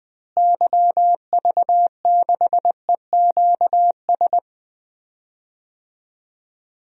Morse code, heard as a keyed tone, about 20 wpm, 700 Hz. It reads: YV6EQS
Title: DXCC-100 Call Signs - 6 Characters - Repeated 3x - 20wpm